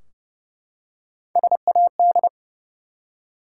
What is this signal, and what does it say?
Morse code, heard as a keyed tone, about 30 wpm, 700 Hz. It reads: SAD
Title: Three Letter Words - 30wpm